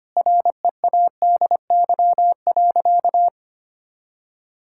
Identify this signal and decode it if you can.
Morse code, heard as a keyed tone, about 25 wpm, 700 Hz. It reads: READY.
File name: practice / 25wpm / Sentences with 2-Word Phrases - 25wpm